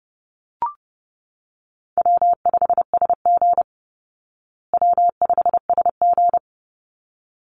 Morse code, heard as a keyed tone, about 30 wpm, 700 Hz. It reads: W5SG W5SG
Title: Call Signs Repeated 3x - 30wpm